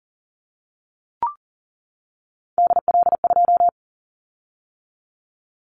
Morse code, heard as a keyed tone, about 40 wpm, 700 Hz. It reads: DL2
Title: Callsign Prefixes for Top 50 DXCC - 40wpm